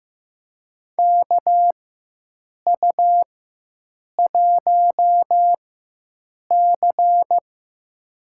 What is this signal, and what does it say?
Morse code, heard as a keyed tone, about 15 wpm, spaced as for 7 wpm, 700 Hz. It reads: KU1C
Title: USA Call Signs - 4x Character Spacing - 15wpm